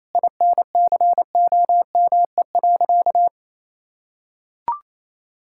Morse code, heard as a keyed tone, about 28 wpm, 700 Hz. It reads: INCOME.
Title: Sentences from Top 1000 Words - Review Part 5 - 28wpm